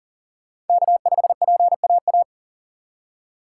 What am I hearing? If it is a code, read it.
Morse code, heard as a keyed tone, about 50 wpm, 700 Hz. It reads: K5PAA
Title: Speed Racing - USA 1x3 Call Signs 40wpm